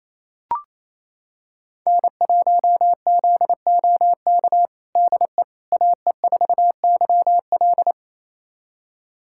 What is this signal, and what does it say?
Morse code, heard as a keyed tone, about 28 wpm, 700 Hz. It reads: N1ZOK DE AE4YL
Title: USA Callsign DE Callsign - 5 Characters - 28wpm